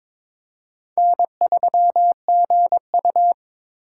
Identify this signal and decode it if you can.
Morse code, heard as a keyed tone, about 23 wpm, 700 Hz. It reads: N3GU
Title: USA Call Signs - 4 Characters - 4x Character Spacing - 22wpm